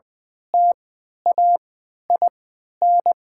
Morse code, heard as a keyed tone, about 20 wpm, spaced as for 11 wpm, 700 Hz. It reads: TAIN
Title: 3x Character Spacing - Top 500 Words - 20wpm